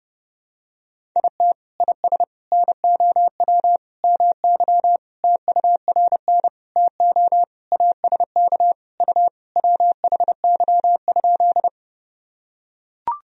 Morse code, heard as a keyed tone, about 32 wpm, 700 Hz. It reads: IT IS NOW MY TURN TO ASK U WHY?
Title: Sentences from Top 200 Words and Abbreviations 30wpm